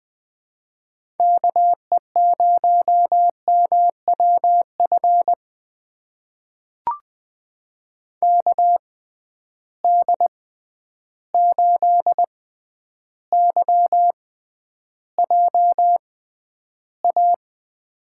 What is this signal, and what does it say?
Morse code, heard as a keyed tone, about 20 wpm, spaced as for 10 wpm, 700 Hz. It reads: KE0MWF KD8YJA
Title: USA Call Signs - 6 Characters - 6x Character Spacing - 20wpm